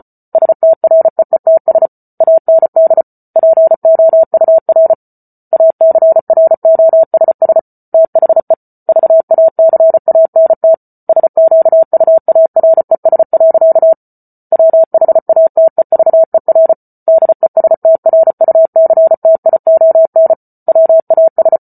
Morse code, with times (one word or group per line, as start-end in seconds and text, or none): 0.3-1.9 STREETS
2.2-3.0 AND
3.4-4.9 POUR
5.5-7.6 ACROSS
7.9-8.5 THE
8.9-10.7 VACANT
11.1-13.9 SQUARES.
14.5-16.7 WHATEVER
17.1-20.3 DESTRUCTION
20.7-21.6 WAS